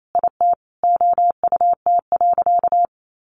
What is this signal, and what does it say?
Morse code, heard as a keyed tone, about 28 wpm, 700 Hz. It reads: IT OUT.
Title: Sentences from Top 1400 Words 28wpm